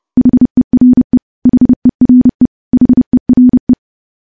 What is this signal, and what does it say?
Morse code, heard as a keyed tone, about 28 wpm, 250 Hz. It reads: HERE HERE HERE
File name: morse / morse-training-30WPM